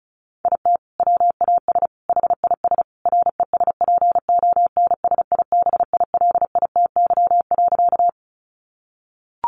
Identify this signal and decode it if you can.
Morse code, heard as a keyed tone, about 34 wpm, 700 Hz. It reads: IT WAS HIS RESPONSIBILITY.
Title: Sentences from Top 1200 Words 35wpm